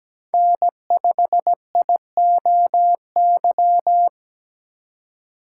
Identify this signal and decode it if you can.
Morse code, heard as a keyed tone, about 17 wpm, 700 Hz. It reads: N5IOY
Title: USA Call Signs - 5 Characters - 3x Character Spacing - 17wpm